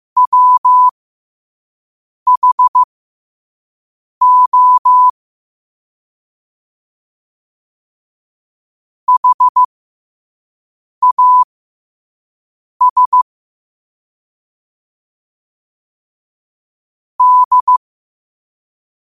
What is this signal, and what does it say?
Morse code, heard as a keyed tone, about 15 wpm, spaced as for 5 wpm, 1000 Hz. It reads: WHO HAS D